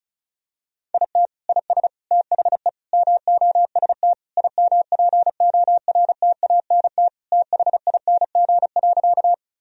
Morse code, heard as a keyed tone, about 35 wpm, 700 Hz. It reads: IT IS THE MOST IMPORTANT THING.
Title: Sentences from Top 1000 Words - Review Part 1 - 35wpm